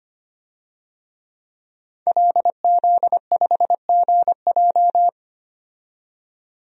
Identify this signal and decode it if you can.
Morse code, heard as a keyed tone, about 25 wpm, 700 Hz. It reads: LZ5GJ
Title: DXCC-100 Call Signs - 5 Characters - 25wpm